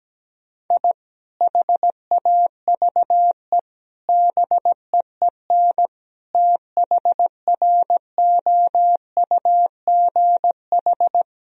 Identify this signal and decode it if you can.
Morse code, heard as a keyed tone, about 17 wpm, 700 Hz. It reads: I HAVE BEEN THROUGH
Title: Sentences from Top 1000 Words - Review Part 4 - 17wpm